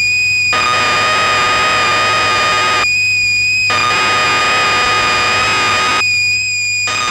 Cassette data (Kansas City type).
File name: magazines / Acorn User / 1985-01 BBC Programs